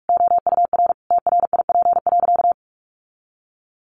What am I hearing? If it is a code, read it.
Morse code, heard as a keyed tone, about 44 wpm, 700 Hz. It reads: OUR TRIP.
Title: Sentences from Top 1300 Words 45wpm